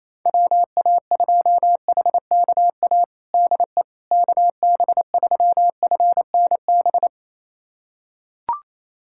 Morse code, heard as a keyed tone, about 28 wpm, 700 Hz. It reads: WA2HKA DE KB3FNB
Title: USA Callsign DE Callsign - 6 Characters - 28wpm